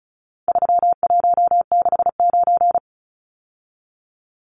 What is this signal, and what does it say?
Morse code, heard as a keyed tone, about 34 wpm, 700 Hz. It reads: 3169
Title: Numbers - 4-Digit 35wpm